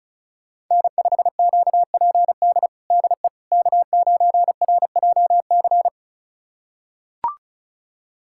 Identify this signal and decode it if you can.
Morse code, heard as a keyed tone, about 34 wpm, 700 Hz. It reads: N5QPD DE K9RJC
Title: USA Callsign DE Callsign - 5 Characters - 35wpm